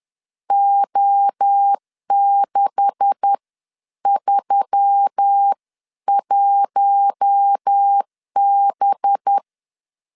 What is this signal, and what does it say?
Morse code, heard as a keyed tone, about 10 wpm, spaced as for 9 wpm, 800 Hz. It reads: O631B